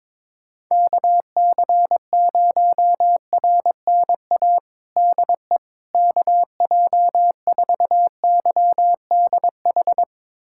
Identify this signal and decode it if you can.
Morse code, heard as a keyed tone, about 22 wpm, 700 Hz. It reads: KC0RNA DE KJ4YDH
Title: USA Callsign DE Callsign 22wpm